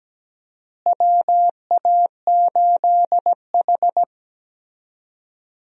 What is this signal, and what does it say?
Morse code, heard as a keyed tone, about 17 wpm, 700 Hz. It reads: WA8H
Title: USA Call Signs - 4 Characters - Repeated 3x - 17wpm